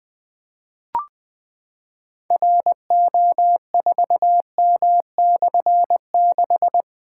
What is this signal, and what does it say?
Morse code, heard as a keyed tone, about 20 wpm, 700 Hz. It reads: RO4M/6
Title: Difficult Call Signs 20wpm